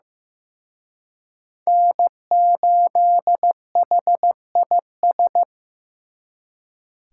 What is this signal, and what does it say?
Morse code, heard as a keyed tone, about 18 wpm, 700 Hz. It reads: N8HIS
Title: Speed Racing - USA 1x3 Call Signs 15wpm